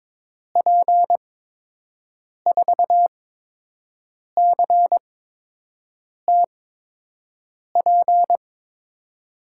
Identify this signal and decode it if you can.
Morse code, heard as a keyed tone, about 22 wpm, spaced as for 6 wpm, 700 Hz. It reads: P4CTP
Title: USA Call Signs - 6 Characters - 8x Character Spacing - 22wpm